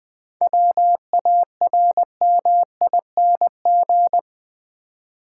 Morse code, heard as a keyed tone, about 20 wpm, 700 Hz. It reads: WARMING
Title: 7 Letter Words - First 3 Letters Given - 20wpm